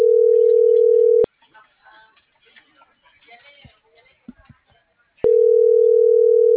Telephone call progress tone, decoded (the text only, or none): ringback tone